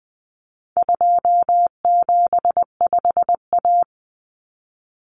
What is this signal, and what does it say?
Morse code, heard as a keyed tone, about 19 wpm, 700 Hz. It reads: 275A
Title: Common Cut Numbers - 4-Digit 20wpm